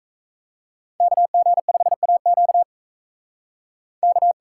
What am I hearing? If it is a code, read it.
Morse code, heard as a keyed tone, about 44 wpm, 700 Hz. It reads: KG5AQ K